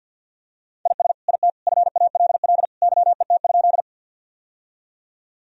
Morse code, heard as a keyed tone, about 52 wpm, 700 Hz. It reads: IS IT FALL YET?
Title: Sentences from Top 1000 Words - Review Part 2 - 50wpm